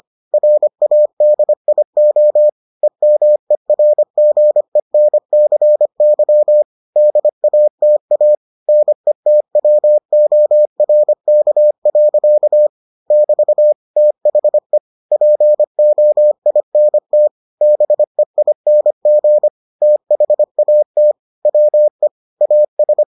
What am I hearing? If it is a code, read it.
Morse code, heard as a keyed tone, about 25 wpm, 600 Hz. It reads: RADIO EMERGENCY DATA NETWORK. <BT> THE POINT BEING THAT WE AS